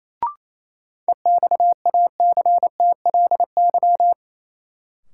Morse code, heard as a keyed tone, about 28 wpm, 700 Hz. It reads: EXACTLY
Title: Sentences from Top 900 Words - New Word Drill - 28wpm